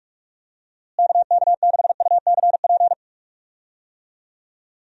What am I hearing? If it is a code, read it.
Morse code, heard as a keyed tone, about 46 wpm, 700 Hz. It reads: KK6UCP
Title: USA Call Signs - 6 Characters - Repeated 2x - 45wpm